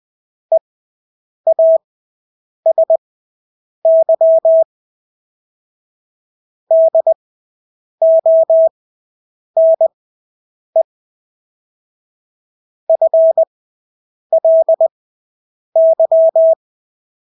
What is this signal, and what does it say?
Morse code, heard as a keyed tone, about 20 wpm, spaced as for 8 wpm, 650 Hz. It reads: EASY DONE FLY
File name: Engelske ord 20/8 wpm